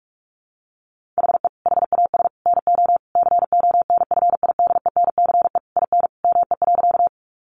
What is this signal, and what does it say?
Morse code, heard as a keyed tone, about 44 wpm, 700 Hz. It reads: HE HAS NO CONFIDENCE IN ME.
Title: Sentences from Top 1500 Words 45wpm